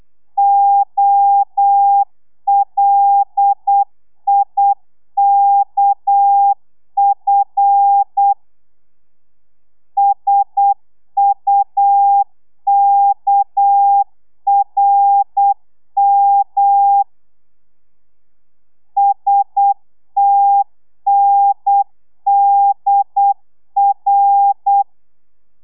Morse code, heard as a keyed tone, about 8 wpm, 800 Hz. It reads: OLIKF SUKRM STNDR
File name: telegrafia / i2tfj / Lez6 35